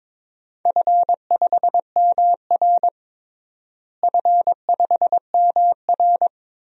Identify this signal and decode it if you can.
Morse code, heard as a keyed tone, about 22 wpm, 700 Hz. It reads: F5MR F5MR
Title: DXCC-100 Call Signs - 4 Characters - Repeated 3x - 22wpm